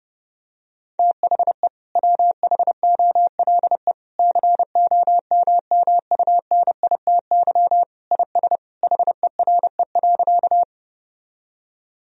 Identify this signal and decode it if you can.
Morse code, heard as a keyed tone, about 32 wpm, 700 Hz. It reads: THE WHOLE COMMUNITY IS HERE.
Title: Sentences from Top 1000 Words - Review Part 4 - 30wpm